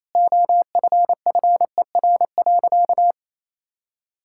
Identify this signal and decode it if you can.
Morse code, heard as a keyed tone, about 27 wpm, 700 Hz. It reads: OFFER.
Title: Sentences from Top 1600 Words 28wpm